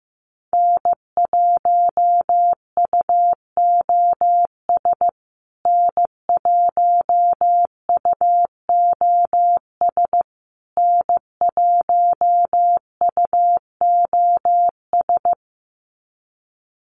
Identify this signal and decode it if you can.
Morse code, heard as a keyed tone, about 15 wpm, 700 Hz. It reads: N1UOS N1UOS N1UOS
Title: USA Call Signs Repeated 3x 15wpm